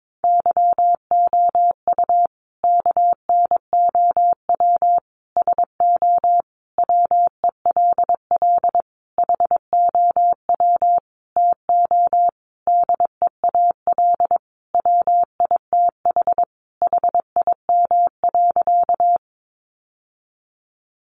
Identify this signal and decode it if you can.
Morse code, heard as a keyed tone, about 22 wpm, 700 Hz. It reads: YOU KNOW SO WELL HOW TO DEAL WITH HIM.
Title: Sentences from Top 800 Words 22wpm